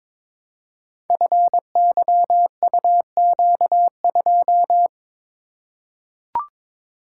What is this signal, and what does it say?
Morse code, heard as a keyed tone, about 22 wpm, 700 Hz. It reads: FYUQ2
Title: Code Groups - 5 Letter:Numbers - 22wpm